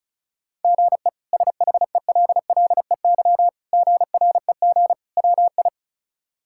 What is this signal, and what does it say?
Morse code, heard as a keyed tone, about 36 wpm, 700 Hz. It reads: GE SHELLEY GREG WI
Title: K1USN SST QSO Practice - 35wpm